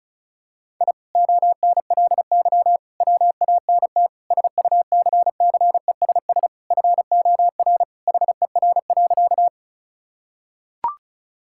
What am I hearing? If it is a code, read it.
Morse code, heard as a keyed tone, about 34 wpm, 700 Hz. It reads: I ONLY WANT SUCCESS FOR HER.
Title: Sentences from Top 1000 Words - Review Part 6 - 35wpm